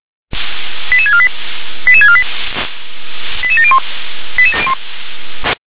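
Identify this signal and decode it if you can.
Five-tone selective call (ZVEI), heard as quick runs of five tones, tones of about 70 ms. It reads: 88648 88648 88701
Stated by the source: ZVEI five-tone selective call received by SDR.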